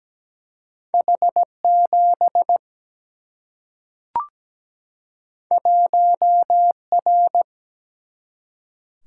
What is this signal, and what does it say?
Morse code, heard as a keyed tone, about 17 wpm, 700 Hz. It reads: H7 1R
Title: Single-Letter Single-Number 17wpm